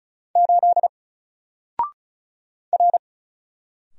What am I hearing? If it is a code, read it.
Morse code, heard as a keyed tone, about 34 wpm, 700 Hz. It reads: 8 R
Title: Single Letter-Number 35wpm